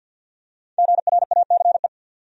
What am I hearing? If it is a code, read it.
Morse code, heard as a keyed tone, about 50 wpm, 700 Hz. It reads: GRACE
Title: USA Common Names - Max 5 Letters - 50wpm